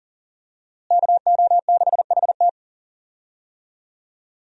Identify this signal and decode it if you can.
Morse code, heard as a keyed tone, about 40 wpm, 700 Hz. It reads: KO6HT
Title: USA Call Signs - 5 Characters - Repeated 2x - 40wpm